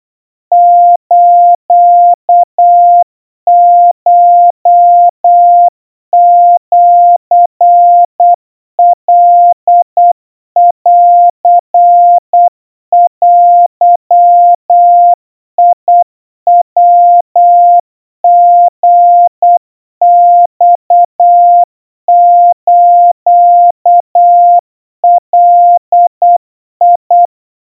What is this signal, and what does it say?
Morse code (Japanese Wabun code), heard as a keyed tone, about 8 wpm, 700 Hz. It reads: スコシカンデヤリマスガ
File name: kawa / Morse sample / kamyu 30 cpm 700 Hz Crispy